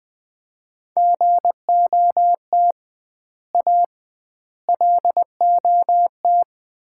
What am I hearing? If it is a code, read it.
Morse code, heard as a keyed tone, about 20 wpm, 700 Hz. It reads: GOT A LOT
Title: 2x Word-Spacing - Sets of 4 Words - 20wpm